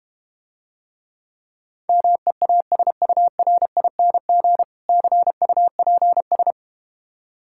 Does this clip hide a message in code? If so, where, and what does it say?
Yes: Morse code from 1.9 to 6.5 s, heard as a keyed tone, about 32 wpm, 700 Hz. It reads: MEASURING CUPS